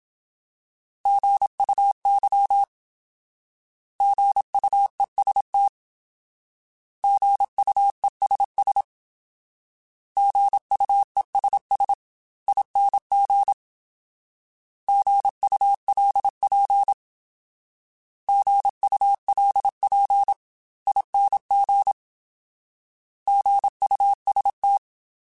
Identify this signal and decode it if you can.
Morse code, heard as a keyed tone, about 27 wpm, 800 Hz. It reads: GUY GUEST GUESS GUESSING GULP GULPING GUST
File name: Alphabetical / Alphabetical mp3s / Gs @25